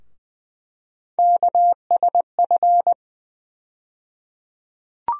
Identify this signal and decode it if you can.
Morse code, heard as a keyed tone, about 20 wpm, 700 Hz. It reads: KSF
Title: USA Call Sign Postfixes - 3 Letters 20wpm